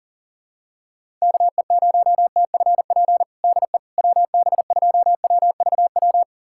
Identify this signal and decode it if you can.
Morse code, heard as a keyed tone, about 40 wpm, 700 Hz. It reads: KE0TFP DE WB2WVW
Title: USA Callsign DE Callsign - 6 Characters - 40wpm